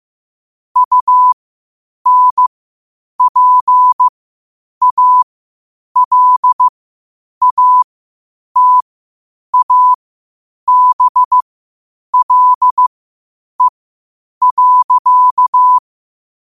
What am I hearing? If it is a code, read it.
Morse code, heard as a keyed tone, about 15 wpm, spaced as for 8 wpm, 1000 Hz. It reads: UNPALATABLE.
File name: qotdfiles / 8wpm